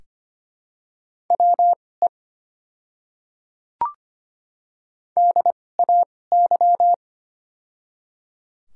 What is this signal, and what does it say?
Morse code, heard as a keyed tone, about 25 wpm, spaced as for 18 wpm, 700 Hz. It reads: WE DAY
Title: 2x Character Spacing - Top 100 Words - 25wpm